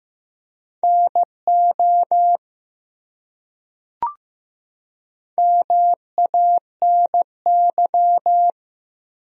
Morse code, heard as a keyed tone, about 15 wpm, 700 Hz. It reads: NO MANY
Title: Top 100 Words 15wpm